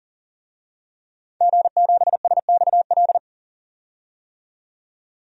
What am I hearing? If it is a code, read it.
Morse code, heard as a keyed tone, about 40 wpm, 700 Hz. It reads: G7SXL
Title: DXCC-100 Call Signs - 5 Characters - 40wpm